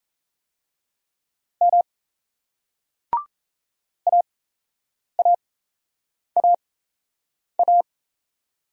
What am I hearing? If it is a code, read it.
Morse code, heard as a keyed tone, about 38 wpm, spaced as for 8 wpm, 700 Hz. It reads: M AAAA